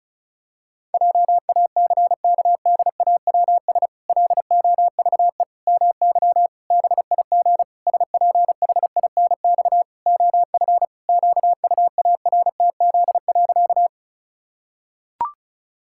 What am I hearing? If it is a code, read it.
Morse code, heard as a keyed tone, about 34 wpm, 700 Hz. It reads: JACKDAWS LOVE MY BIG SPHINX OF QUARTZ.
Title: Pangram Challenge v2 - 35wpm